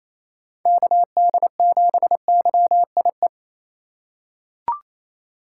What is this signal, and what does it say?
Morse code, heard as a keyed tone, about 28 wpm, 700 Hz. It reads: KD7YIE
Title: USA Call Signs - 6 Characters - 4x Character Spacing - 28wpm